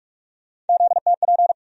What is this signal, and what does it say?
Morse code, heard as a keyed tone, about 42 wpm, 700 Hz. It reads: ZTP
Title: USA Call Sign Postfixes - 3 Letters 45wpm